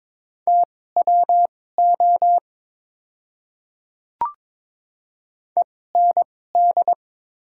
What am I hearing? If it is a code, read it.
Morse code, heard as a keyed tone, about 22 wpm, spaced as for 16 wpm, 690 Hz. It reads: TWO END
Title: 2x Character Spacing - Top 300 Words - 22wpm